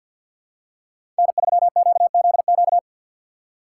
Morse code, heard as a keyed tone, about 52 wpm, 700 Hz. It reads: N3QZQ
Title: USA Call Signs - 5 Characters - 4x Character Spacing - 50wpm